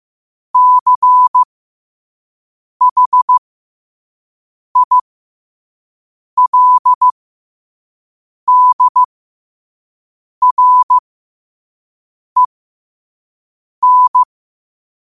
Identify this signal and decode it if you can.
Morse code, heard as a keyed tone, about 15 wpm, spaced as for 5 wpm, 1000 Hz. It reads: CHILDREN
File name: qotdfiles / 5wpm